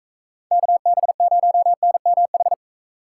Morse code, heard as a keyed tone, about 37 wpm, 700 Hz. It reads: KB0NMH